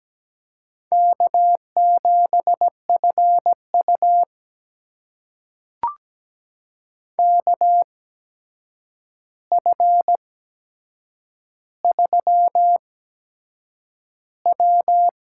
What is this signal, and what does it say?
Morse code, heard as a keyed tone, about 17 wpm, spaced as for 7 wpm, 700 Hz. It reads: K7FU KF3W
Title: USA Call Signs - 4 Characters - 8x Character Spacing - 17wpm